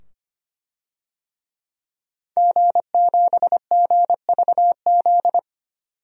Morse code, heard as a keyed tone, about 25 wpm, 700 Hz. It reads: G7GVZ
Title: DXCC-100 Call Signs - 5 Characters - 25wpm